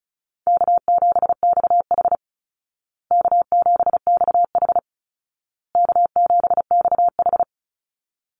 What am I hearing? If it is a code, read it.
Morse code, heard as a keyed tone, about 34 wpm, 700 Hz. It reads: K7XH K7XH K7XH
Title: Easy Call Signs Repeated 3x - 35wpm